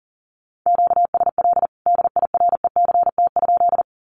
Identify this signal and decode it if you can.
Morse code, heard as a keyed tone, about 40 wpm, 700 Hz. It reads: QSL DIRECT?
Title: Common QSO Phrases 40wpm